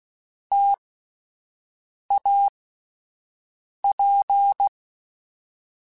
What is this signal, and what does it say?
Morse code, heard as a keyed tone, about 16 wpm, spaced as for 5 wpm, 800 Hz. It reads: TAP
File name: Difficult Combos / Difficult Combos-mp3s / T AOEI @10